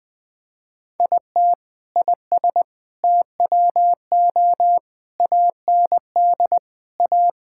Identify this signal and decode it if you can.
Morse code, heard as a keyed tone, about 21 wpm, 700 Hz. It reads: IT IS TWO AND A
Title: Sentences with 4-Word Phrases - Spoken Beginning - 20wpm